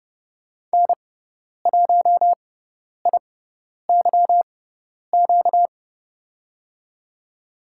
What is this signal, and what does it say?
Morse code, heard as a keyed tone, about 30 wpm, spaced as for 10 wpm, 700 Hz. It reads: N1IYQ
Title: USA Call Signs - 5 Characters - 6x Character Spacing - 30wpm